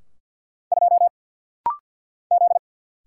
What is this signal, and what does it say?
Morse code, heard as a keyed tone, about 48 wpm, 700 Hz. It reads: 2 Z
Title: Single Letter-Number - Rapid-Fire 50wpm